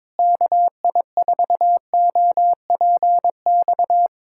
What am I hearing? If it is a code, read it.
Morse code, heard as a keyed tone, about 22 wpm, 700 Hz. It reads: KI4OPX
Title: USA Callsign DE Callsign 22wpm